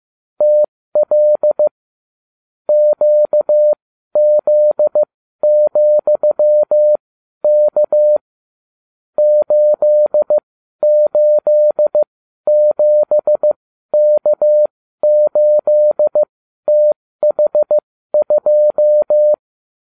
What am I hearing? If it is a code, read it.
Morse code, heard as a keyed tone, about 15 wpm, spaced as for 12 wpm, 600 Hz. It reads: TL QZ,K 887K8TH2